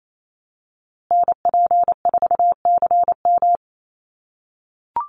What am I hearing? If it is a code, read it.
Morse code, heard as a keyed tone, about 28 wpm, 700 Hz. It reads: NP4CM
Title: USA Call Signs Repeated 3x 28wpm